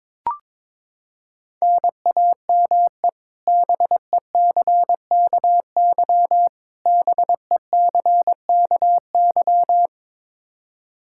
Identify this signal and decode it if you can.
Morse code, heard as a keyed tone, about 22 wpm, 700 Hz. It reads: NAME BECKY BECKY
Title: K1USN SST QSO Practice 22wpm